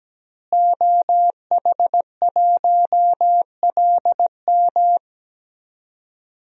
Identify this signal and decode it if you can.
Morse code, heard as a keyed tone, about 17 wpm, 700 Hz. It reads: OH1LM
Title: DXCC-100 Call Signs - 5 Characters - Repeated 2x - 17wpm